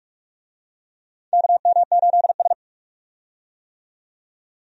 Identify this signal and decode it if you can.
Morse code, heard as a keyed tone, about 46 wpm, 700 Hz. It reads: KM8S
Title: USA Call Signs - 4 Characters - 45wpm